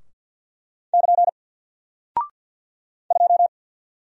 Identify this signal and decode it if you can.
Morse code, heard as a keyed tone, about 48 wpm, 700 Hz. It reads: <KN> 2
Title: Single Character (Letter:Number:Punctuation:Prosign) 50wpm